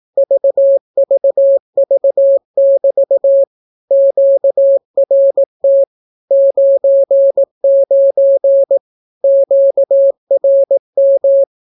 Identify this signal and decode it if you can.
Morse code, heard as a keyed tone, about 18 wpm, 550 Hz. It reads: VVV<BT> QRT 99 QRM
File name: naukapodstaw / skroty / skroty